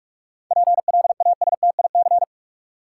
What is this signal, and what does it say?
Morse code, heard as a keyed tone, about 47 wpm, 700 Hz. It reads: PLASTIC
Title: Seven Letter Words - 45wpm